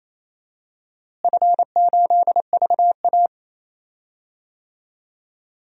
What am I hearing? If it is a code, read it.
Morse code, heard as a keyed tone, about 28 wpm, 700 Hz. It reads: F8VA
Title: DXCC-100 Call Signs - 4 Characters - Repeated 2x - 28wpm